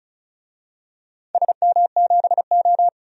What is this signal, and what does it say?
Morse code, heard as a keyed tone, about 36 wpm, 700 Hz. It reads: SM7O
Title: DXCC-100 Call Signs - 4 Characters - 35wpm